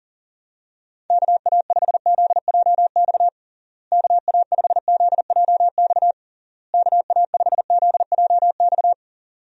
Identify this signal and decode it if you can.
Morse code, heard as a keyed tone, about 40 wpm, 700 Hz. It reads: KA5ZJX KA5ZJX KA5ZJX
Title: USA Call Signs - 6 Characters - Repeated 3x - 40wpm